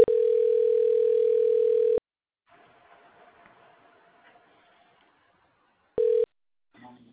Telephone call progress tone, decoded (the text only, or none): ringback tone